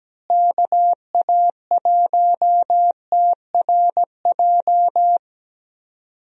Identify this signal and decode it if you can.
Morse code, heard as a keyed tone, about 17 wpm, 700 Hz. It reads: KA1TRJ
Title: USA Call Signs - 6 Characters - Repeated 2x - 17wpm